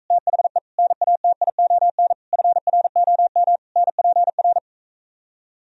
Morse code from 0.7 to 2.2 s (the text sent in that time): NATION